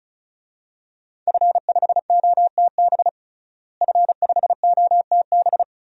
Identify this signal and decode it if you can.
Morse code, heard as a keyed tone, about 34 wpm, 700 Hz. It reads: F5OTB F5OTB
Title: DXCC-100 Call Signs - 5 Characters - Repeated 3x - 35wpm